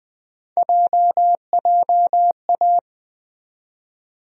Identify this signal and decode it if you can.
Morse code, heard as a keyed tone, about 20 wpm, 700 Hz. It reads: JJA
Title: USA Call Sign Postfixes - 3 Letters 20wpm